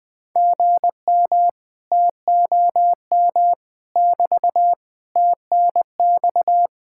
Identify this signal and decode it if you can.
Morse code, heard as a keyed tone, about 20 wpm, 700 Hz. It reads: GM TOM <BT> TNX
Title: Speed-Racing Common QSO Phrases Encore - 17wpm